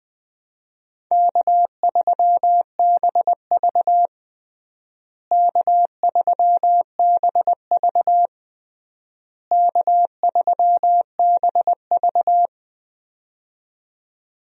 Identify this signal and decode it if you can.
Morse code, heard as a keyed tone, about 20 wpm, 700 Hz. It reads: K3BV K3BV K3BV
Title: USA Call Signs - 4 Characters - Repeated 3x - 20wpm